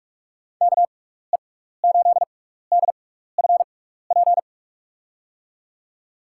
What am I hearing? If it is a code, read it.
Morse code, heard as a keyed tone, about 41 wpm, spaced as for 15 wpm, 700 Hz. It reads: KE8DFP
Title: USA Call Signs - 6 Characters - 6x Character Spacing - 45wpm